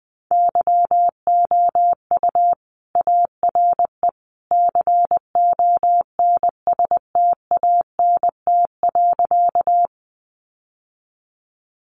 Morse code, heard as a keyed tone, about 20 wpm, 700 Hz. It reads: YOU ARE CONSTANT.
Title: Sentences from Top 1500 Words 20wpm